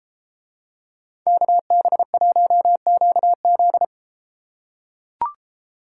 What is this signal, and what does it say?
Morse code, heard as a keyed tone, about 33 wpm, 700 Hz. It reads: KB1QZ